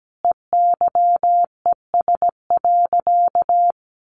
Morse code, heard as a keyed tone, about 17 wpm, 700 Hz. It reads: EYES.
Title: Easy Phrases 17wpm